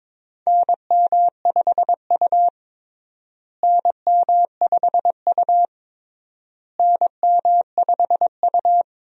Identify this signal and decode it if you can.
Morse code, heard as a keyed tone, about 22 wpm, 700 Hz. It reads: NM5U NM5U NM5U
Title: USA Call Signs - 4 Characters - Repeated 3x - 22wpm